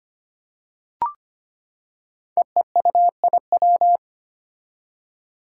Morse code, heard as a keyed tone, about 25 wpm, 700 Hz. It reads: EEUIW
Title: Five Challenging Letter-Number Combinations 25wpm